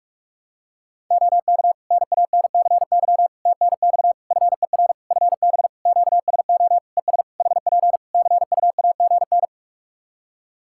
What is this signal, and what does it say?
Morse code, heard as a keyed tone, about 46 wpm, 700 Hz. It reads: OK NANCY TNX FER FB QSO ES HP CUAGN
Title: Practice QSO - Exchange 4 - Ending - 45wpm